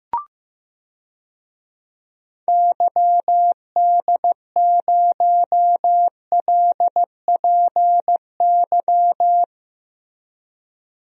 Morse code, heard as a keyed tone, about 15 wpm, 700 Hz. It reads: YD0LPY
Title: DXCC-100 Call Signs - 6 Characters - 15wpm